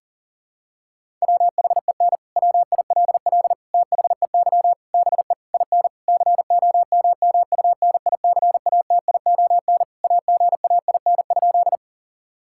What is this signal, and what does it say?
Morse code, heard as a keyed tone, about 40 wpm, 700 Hz. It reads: WHEN WILL THEY BE IN COMMUNICATION AGAIN?
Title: Sentences from Top 1000 Words - Review Part 6 - 40wpm